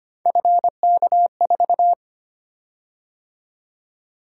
Morse code, heard as a keyed tone, about 25 wpm, 700 Hz. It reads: FK4
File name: Three Letter-Number Combinations - 25wpm